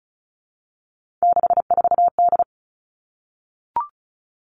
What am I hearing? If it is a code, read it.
Morse code, heard as a keyed tone, about 36 wpm, 700 Hz. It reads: 64D
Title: Three Challenging Letter-Number Combinations 35wpm